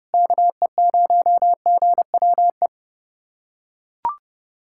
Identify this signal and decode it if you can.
Morse code, heard as a keyed tone, about 32 wpm, 700 Hz. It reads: KE0GWE
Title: USA Callsign DE Callsign 30wpm